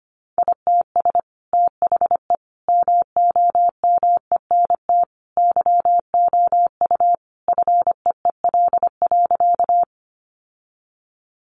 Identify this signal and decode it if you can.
Morse code, heard as a keyed tone, about 25 wpm, 700 Hz. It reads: ITS THE MOMENT YOU FEEL.
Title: Sentences from Top 1000 Words 25wpm